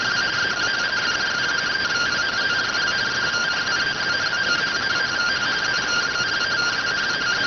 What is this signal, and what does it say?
Two-tone teleprinter signal, about 170 Hz shift.